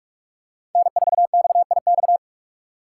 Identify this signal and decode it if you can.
Morse code, heard as a keyed tone, about 43 wpm, 700 Hz. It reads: N4XIX
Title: USA Call Signs - 5 Characters - 2x Character Spacing - 45wpm